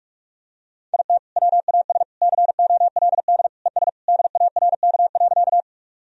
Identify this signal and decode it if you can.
Morse code, heard as a keyed tone, about 42 wpm, 700 Hz. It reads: IT WAS COLD ES DARK.
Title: Sentences from Top 500 Words and Abbreviations 45wpm